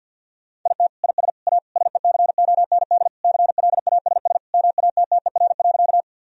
Morse code, heard as a keyed tone, about 50 wpm, 700 Hz. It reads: IT IS A SECOND CLASS MATTER.
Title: Sentences from Top 1000 Words - Review Part 2 - 50wpm